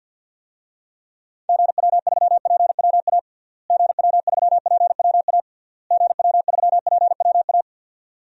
Morse code, heard as a keyed tone, about 53 wpm, 700 Hz. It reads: GW3PWA GW3PWA GW3PWA
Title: DXCC-100 Call Signs - 6 Characters - Repeated 3x - 50wpm